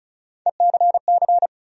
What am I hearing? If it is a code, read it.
Morse code, heard as a keyed tone, about 36 wpm, 700 Hz. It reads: ECC
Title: USA Call Sign Postfixes - 3 Letters 35wpm